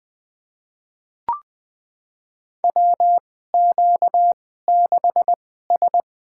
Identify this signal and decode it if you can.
Morse code, heard as a keyed tone, about 20 wpm, spaced as for 14 wpm, 700 Hz. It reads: WQ6S
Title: USA Call Signs - 2x Character Spacing - 20wpm